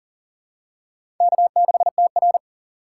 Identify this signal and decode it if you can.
Morse code, heard as a keyed tone, about 40 wpm, 700 Hz. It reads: K6TR
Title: USA Call Signs - 4 Characters - 8x Character Spacing - 40wpm